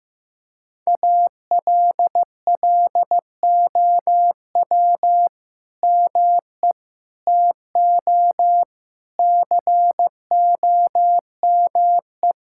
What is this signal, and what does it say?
Morse code, heard as a keyed tone, about 15 wpm, 700 Hz. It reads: ALLOW ME TO COME
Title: Sentences from Top 1000 Words - Review Part 5 - 15wpm